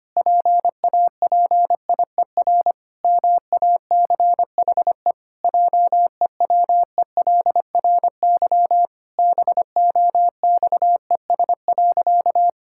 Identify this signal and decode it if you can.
Morse code, heard as a keyed tone, about 25 wpm, 700 Hz. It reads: PAPIER MACHE JEWELRY BOXES.
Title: Pangram Challenge v2 - 25wpm